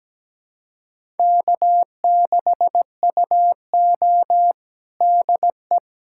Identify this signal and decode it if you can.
Morse code, heard as a keyed tone, about 17 wpm, 700 Hz. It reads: K6UO DE
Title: USA Callsign DE Callsign - 4 Characters - 17wpm